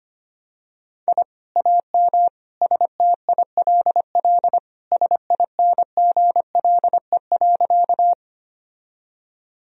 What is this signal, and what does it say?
Morse code, heard as a keyed tone, about 25 wpm, 700 Hz. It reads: I AM STILL SINGLE.
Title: Sentences from Top 1000 Words - Review Part 4 - 25wpm